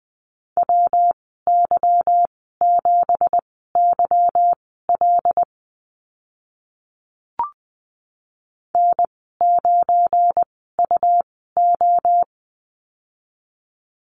Morse code, heard as a keyed tone, about 20 wpm, spaced as for 15 wpm, 700 Hz. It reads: WY7YL N9UO
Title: 173 Lesson 28g - Number 9 - Call Signs 20wpm